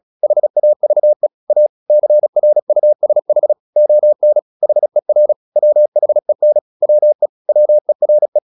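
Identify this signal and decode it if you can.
Morse code, heard as a keyed tone, about 36 wpm, 600 Hz. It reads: HAVE A CRUSH ON HER WHEN WE WERE
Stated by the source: Morse practice at 36 wpm, text of Ken Follett's novel Never, from CW med Gnister.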